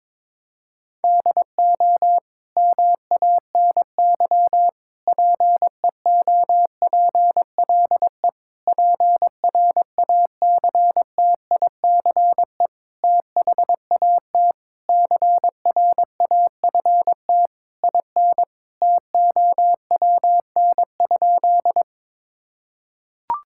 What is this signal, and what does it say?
Morse code, heard as a keyed tone, about 22 wpm, 700 Hz. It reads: DO MANY PEOPLE PRACTICE THAT CRAFT IN TOWN?
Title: Sentences from Top 1000 Words - Review Part 6 - 22wpm